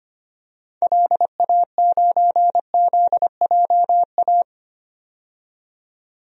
Morse code, heard as a keyed tone, about 25 wpm, 700 Hz. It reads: LA9ZJA
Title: DXCC-100 Call Signs - 6 Characters - 25wpm